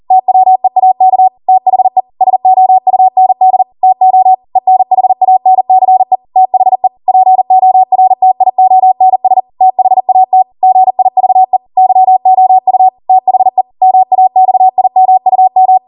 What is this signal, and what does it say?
Morse code, heard as a keyed tone, about 40 wpm, 750 Hz. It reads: TWEAK THE SOUND TO ENHANCE THE PORTIONS THAT GIVE YOU THE MAXIMUM